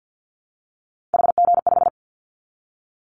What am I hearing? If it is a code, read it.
Morse code, heard as a keyed tone, about 50 wpm, 700 Hz. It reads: HG5
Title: Callsign Prefixes for Top 50 DXCC - 50wpm